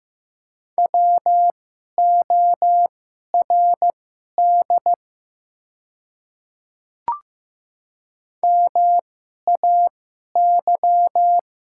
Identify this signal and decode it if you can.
Morse code, heard as a keyed tone, about 15 wpm, spaced as for 11 wpm, 700 Hz. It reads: WORD MAY
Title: 2x Character Spacing - Top 200 Words - 15wpm